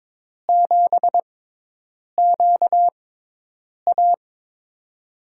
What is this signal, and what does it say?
Morse code, heard as a keyed tone, about 22 wpm, spaced as for 8 wpm, 700 Hz. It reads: 7QA
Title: USA Call Signs - 6x Character Spacing - 22wpm